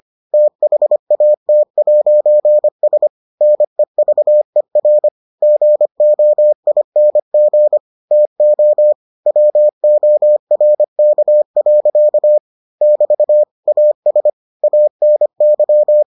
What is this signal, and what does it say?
Morse code, heard as a keyed tone, about 25 wpm, 600 Hz. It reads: THAT'S NEVER GOING TO WORK. <BT> AS ANY